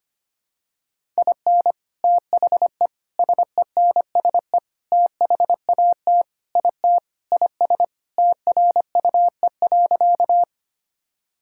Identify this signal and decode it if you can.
Morse code, heard as a keyed tone, about 25 wpm, 700 Hz. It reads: IN THE SENSE THAT IT IS TRUE.
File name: practice / 25wpm / Sentences with 4-Word Phrases - 25wpm